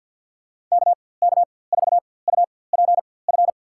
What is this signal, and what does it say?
Morse code, heard as a keyed tone, about 52 wpm, spaced as for 23 wpm, 700 Hz. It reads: KK4UPF
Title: USA Call Signs - 6 Characters - 4x Character Spacing - 50wpm